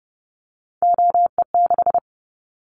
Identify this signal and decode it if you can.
Morse code, heard as a keyed tone, about 30 wpm, 700 Hz. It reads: OE6
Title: Callsign Prefixes for Top 100 DXCC - 30wpm